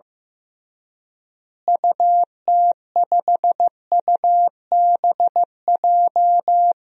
Morse code, heard as a keyed tone, about 15 wpm, 700 Hz. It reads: UT5UBJ
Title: DXCC-100 Call Signs - 6 Characters - Repeated 3x - 15wpm